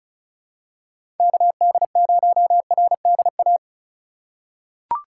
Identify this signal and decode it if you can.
Morse code, heard as a keyed tone, about 36 wpm, 700 Hz. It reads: KD0RDA
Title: USA Call Signs - 6 Characters - 3x Character Spacing - 35wpm